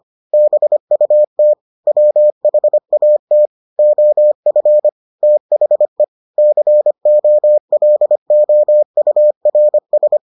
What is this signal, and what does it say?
Morse code, heard as a keyed tone, about 25 wpm, 600 Hz. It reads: BUT WHAT OF THE COLOURS